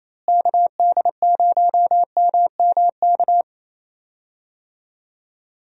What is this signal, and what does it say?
Morse code, heard as a keyed tone, about 28 wpm, 700 Hz. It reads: KD0MMK
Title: USA Call Signs - 6 Characters - Repeated 3x - 28wpm